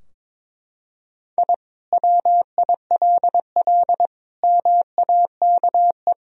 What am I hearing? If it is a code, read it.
Morse code, heard as a keyed tone, about 22 wpm, 700 Hz. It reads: I WILL MAKE
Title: Sentences from Top 1000 Words - Review Part 4 - 22wpm